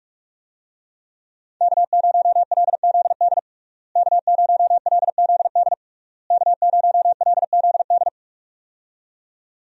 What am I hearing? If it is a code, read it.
Morse code, heard as a keyed tone, about 46 wpm, 700 Hz. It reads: K0LZD K0LZD K0LZD
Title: USA Call Signs - 5 Characters - Repeated 3x - 45wpm